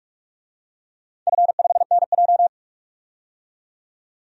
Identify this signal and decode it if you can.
Morse code, heard as a keyed tone, about 44 wpm, 690 Hz. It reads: F5NJ
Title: DXCC-100 Call Signs - 4 Characters - 45wpm